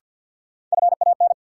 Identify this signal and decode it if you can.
Morse code, heard as a keyed tone, about 53 wpm, 700 Hz. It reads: FAN
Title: Sentences from Top 1300 Words - New Word Drill Alt - 50wpm